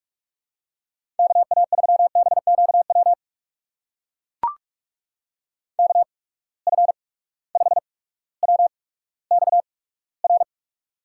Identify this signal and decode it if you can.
Morse code, heard as a keyed tone, about 44 wpm, spaced as for 19 wpm, 700 Hz. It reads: KA3BQW KF5WXR
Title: USA Call Signs - 6 Characters - 8x Character Spacing - 45wpm